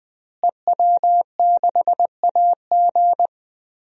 Morse code, heard as a keyed tone, about 19 wpm, 700 Hz. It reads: EW6AG
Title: DXCC-100 Call Signs - 5 Characters - Repeated 3x - 20wpm